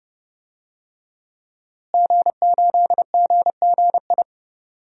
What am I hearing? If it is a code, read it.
Morse code, heard as a keyed tone, about 32 wpm, 700 Hz. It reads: G8GGI
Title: DXCC-100 Call Signs - 5 Characters - Repeated 2x - 30wpm